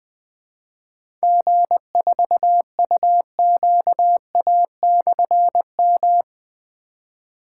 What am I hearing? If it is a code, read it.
Morse code, heard as a keyed tone, about 20 wpm, 700 Hz. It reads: G4UQA/M
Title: Difficult Call Signs 20wpm